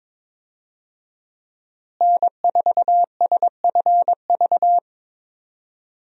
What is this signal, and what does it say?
Morse code, heard as a keyed tone, about 22 wpm, 700 Hz. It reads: N4SFV
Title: USA Call Signs - 5 Characters - Repeated 2x - 22wpm